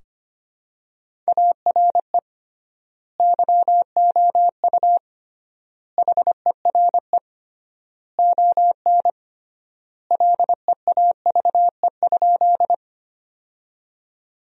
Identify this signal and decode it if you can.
Morse code, heard as a keyed tone, about 25 wpm, 700 Hz. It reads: ARE YOU HERE ON LEAVE?
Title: 3x Word-Spacing - Sentences from Top 400 Words - 25wpm